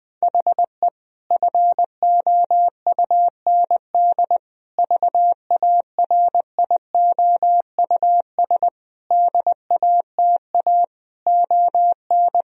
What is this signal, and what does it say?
Morse code, heard as a keyed tone, about 20 wpm, 700 Hz. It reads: HE FOUND VARIOUS DATA ON
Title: Sentences from Top 1000 Words - Review Part 4 - 20wpm